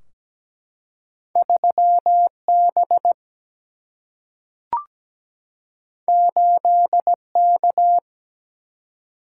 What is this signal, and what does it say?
Morse code, heard as a keyed tone, about 17 wpm, 700 Hz. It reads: 3B 8K
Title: Single-Letter Single-Number 17wpm